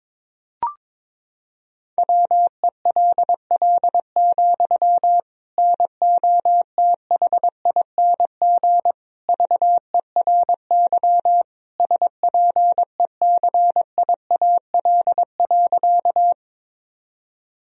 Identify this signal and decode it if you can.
Morse code, heard as a keyed tone, about 22 wpm, 700 Hz. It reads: WELL, NOTHING VERY SPECIAL.
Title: Sentences from Top 1000 Words - Review Part 3 - 22wpm